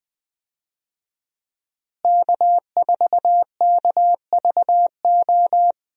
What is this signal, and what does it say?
Morse code, heard as a keyed tone, about 20 wpm, 700 Hz. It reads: K4KVO
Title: USA Call Signs - 5 Characters - Repeated 3x - 20wpm